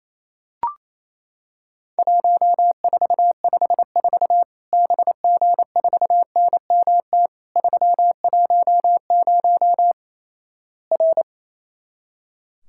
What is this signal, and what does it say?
Morse code, heard as a keyed tone, about 28 wpm, 700 Hz. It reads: 1454 BG4NMT 310 R
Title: Worked All Europe DX Contest - QTC Exchange - 28wpm